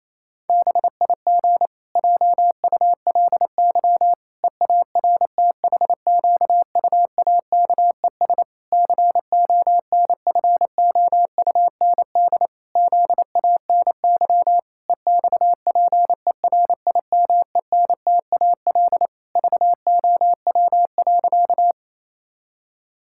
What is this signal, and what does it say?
Morse code, heard as a keyed tone, about 27 wpm, 700 Hz. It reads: BIG JULY EARTHQUAKES CONFOUND ZANY EXPERIMENTAL VOW.
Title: Pangram Challenge v2 - 28wpm